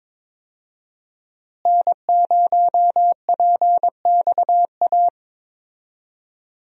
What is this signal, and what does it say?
Morse code, heard as a keyed tone, about 22 wpm, 700 Hz. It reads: N0PXA
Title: USA Call Signs - 5 Characters - Repeated 3x - 22wpm